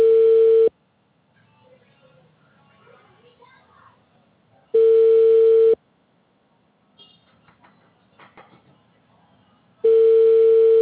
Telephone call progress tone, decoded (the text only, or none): ringback tone